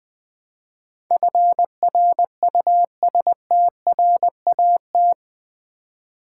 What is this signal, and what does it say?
Morse code, heard as a keyed tone, about 20 wpm, 700 Hz. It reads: FRUSTRAT
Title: Build A Word - 4 Letters At A Time - 11-Letter Words - 20wpm